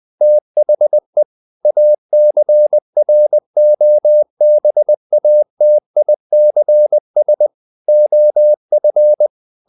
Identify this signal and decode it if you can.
Morse code, heard as a keyed tone, about 20 wpm, 600 Hz. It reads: THE ACROBATICS OF
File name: the old man and the sea part 2 20wpm